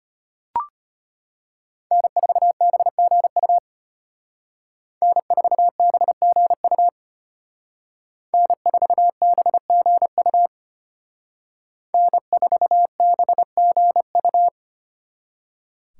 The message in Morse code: N4BGU N4BGU N4BGU N4BGU

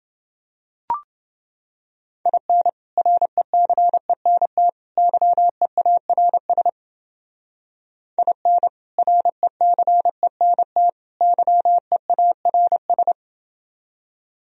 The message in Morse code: IN RECENT YEARS IN RECENT YEARS

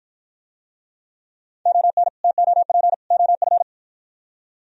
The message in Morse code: ON TOP OF

